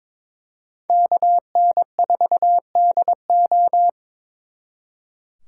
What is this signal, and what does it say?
Morse code, heard as a keyed tone, about 26 wpm, 700 Hz. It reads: KN4DO
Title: Speed Racing - USA 2x2 Call Signs 22wpm